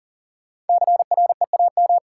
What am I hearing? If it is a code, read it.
Morse code, heard as a keyed tone, about 40 wpm, 700 Hz. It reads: CREAM
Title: Sentences from Top 1500 Words - New Word Drill - 40wpm